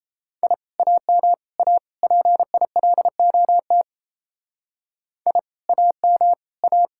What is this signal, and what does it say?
Morse code, heard as a keyed tone, about 34 wpm, 700 Hz. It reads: I AM A PILOT I AM A
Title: Speed-Racing Common QSO Phrases Encore - 28wpm